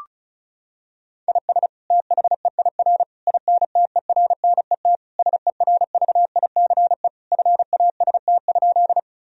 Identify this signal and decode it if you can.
Morse code, heard as a keyed tone, about 34 wpm, 700 Hz. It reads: IS THEIR INTERNET SERVICE FAST?
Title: Sentences from Top 1000 Words - Review Part 4 - 35wpm